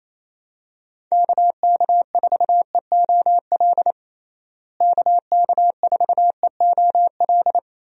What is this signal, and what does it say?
Morse code, heard as a keyed tone, about 28 wpm, 700 Hz. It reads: KK4EOL KK4EOL
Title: USA Call Signs - 6 Characters - Repeated 3x - 28wpm